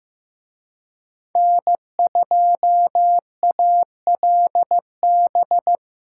Morse code, heard as a keyed tone, about 18 wpm, 700 Hz. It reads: N2ALB
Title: Speed Racing - USA 1x3 Call Signs 15wpm